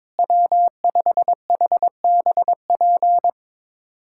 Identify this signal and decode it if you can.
Morse code, heard as a keyed tone, about 22 wpm, 700 Hz. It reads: W5HBP
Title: USA Callsign DE Callsign 22wpm